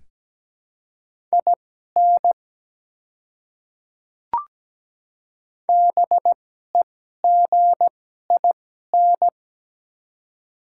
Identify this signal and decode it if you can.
Morse code, heard as a keyed tone, about 17 wpm, spaced as for 12 wpm, 700 Hz. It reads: IN BEGIN
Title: 2x Character Spacing - Top 300 Words - 17wpm